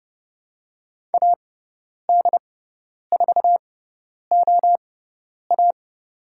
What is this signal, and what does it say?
Morse code, heard as a keyed tone, about 30 wpm, spaced as for 10 wpm, 700 Hz. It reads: AD4OA
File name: Lesson 16g - Letter c - Call Signs 30wpm